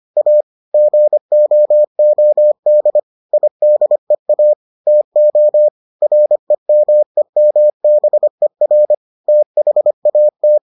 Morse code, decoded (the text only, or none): A GOOD IDEA TO REMEMBER THAT